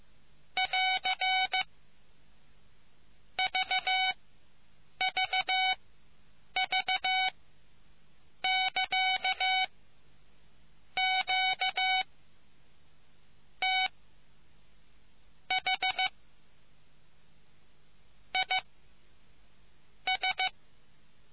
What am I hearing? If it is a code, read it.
Morse code, heard as a keyed tone, about 15 wpm, spaced as for 6 wpm, 750 Hz. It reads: <AR> VVV <KA> QTH IS